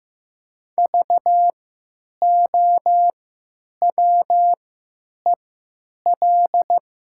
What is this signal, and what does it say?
Morse code, heard as a keyed tone, about 15 wpm, spaced as for 9 wpm, 700 Hz. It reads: VOWEL
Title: 3x Character Spacing - Top 500 Words - 15wpm